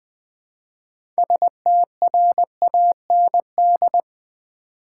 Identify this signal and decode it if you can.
Morse code, heard as a keyed tone, about 20 wpm, 700 Hz. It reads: STRAND
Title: Build A Word - 2 Letters At A Time - 6-Letter Words - 20wpm